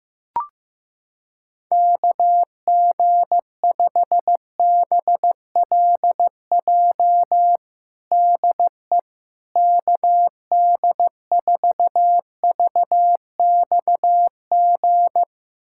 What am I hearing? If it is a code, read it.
Morse code, heard as a keyed tone, about 15 wpm, 700 Hz. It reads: KG5BLJ DE KD4VXG
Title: USA Callsign DE Callsign 15wpm